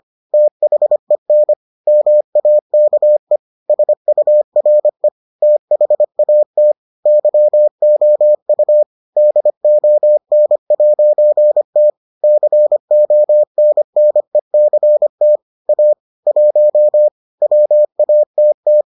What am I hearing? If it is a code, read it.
Morse code, heard as a keyed tone, about 25 wpm, 600 Hz. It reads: THEN MAKE SURE THAT YOU DON'T CONNECT A 1 WATT